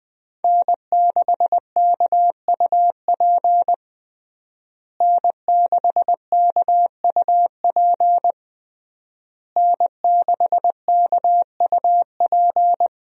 Morse code, heard as a keyed tone, about 20 wpm, 700 Hz. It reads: N6KUP N6KUP N6KUP
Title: USA Call Signs - 5 Characters - Repeated 3x - 20wpm